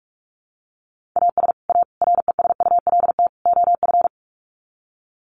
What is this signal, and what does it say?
Morse code, heard as a keyed tone, about 44 wpm, 700 Hz. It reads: AS A RESULT OF
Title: Sets of 4 Words 45wpm